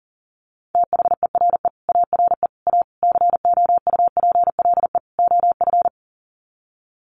Morse code, heard as a keyed tone, about 38 wpm, 700 Hz. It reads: THERE ARE A COUPLE OF
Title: Sets of 5 Words Encore 40wpm